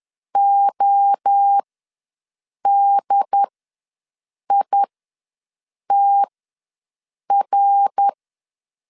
Morse code, heard as a keyed tone, about 10 wpm, spaced as for 6 wpm, 800 Hz. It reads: ODITR
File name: LEZIONI / 6 PKX0 / 6 PK ws 10wpm 3745